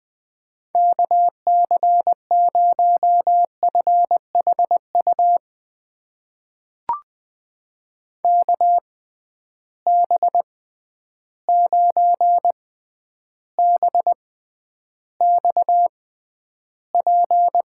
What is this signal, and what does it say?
Morse code, heard as a keyed tone, about 21 wpm, spaced as for 10 wpm, 700 Hz. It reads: KC0FHU KB9BXP
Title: USA Call Signs - 6 Characters - 6x Character Spacing - 20wpm